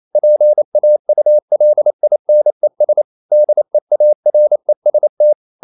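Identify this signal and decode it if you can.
Morse code, heard as a keyed tone, about 28 wpm, 600 Hz. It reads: PAULINES DEAREST
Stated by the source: Morse code practice at 28 wpm, text of Ken Follett's novel Never in English, from CW med Gnister.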